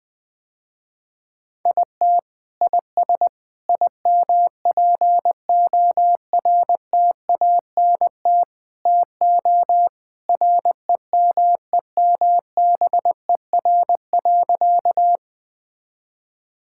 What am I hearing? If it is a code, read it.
Morse code, heard as a keyed tone, about 21 wpm, 690 Hz. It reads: IT IS IMPORTANT TO REMEMBER.
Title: Sentences with 4-Word Phrases - Spoken Beginning - 20wpm